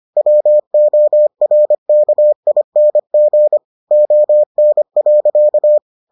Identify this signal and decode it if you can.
Morse code, heard as a keyed tone, about 25 wpm, 600 Hz. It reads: WORKING ON.